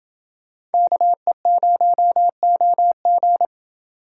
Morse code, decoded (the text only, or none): KE0OG